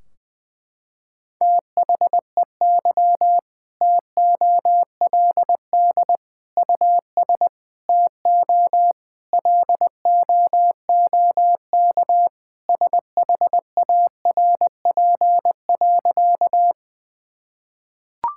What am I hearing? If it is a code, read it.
Morse code, heard as a keyed tone, about 20 wpm, 700 Hz. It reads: THEY TOLD US TO LOOK SHARP.
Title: Sentences from Top 1600 Words 20wpm